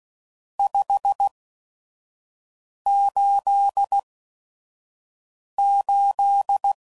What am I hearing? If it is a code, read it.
Morse code, heard as a keyed tone, about 16 wpm, spaced as for 5 wpm, 800 Hz. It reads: 588
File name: CW Code Groups @5WPM / RST's / Various RST No's